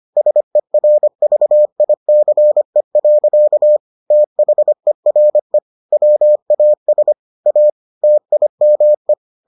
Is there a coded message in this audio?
Yes: Morse code from 0.2 to 9.1 s, heard as a keyed tone, about 25 wpm, 600 Hz. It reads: SERVICE. THERE WAS A TIME